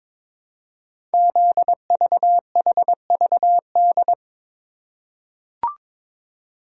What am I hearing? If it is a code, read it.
Morse code, heard as a keyed tone, about 22 wpm, 700 Hz. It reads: ZVHVD
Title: Code Groups - 5 Letters - 22wpm